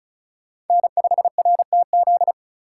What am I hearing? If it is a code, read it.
Morse code, heard as a keyed tone, about 36 wpm, 700 Hz. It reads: N5RTZ
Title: USA Call Signs - 5 Characters - 4x Character Spacing - 35wpm